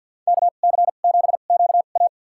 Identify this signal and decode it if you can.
Morse code, heard as a keyed tone, about 47 wpm, spaced as for 35 wpm, 700 Hz. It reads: KC7QA
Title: USA Call Signs - 5 Characters - 2x Character Spacing - 50wpm